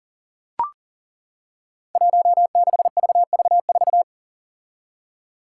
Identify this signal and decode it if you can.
Morse code, heard as a keyed tone, about 40 wpm, 700 Hz. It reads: 16VV4
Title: Five Challenging Letter-Number Combinations 40wpm